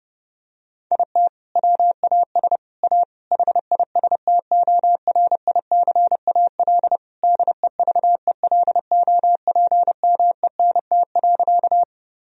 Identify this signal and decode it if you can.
Morse code, heard as a keyed tone, about 30 wpm, 700 Hz. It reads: IT WAS A HISTORICAL DEVELOPMENT.
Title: Sentences from Top 1000 Words - Review Part 4 - 30wpm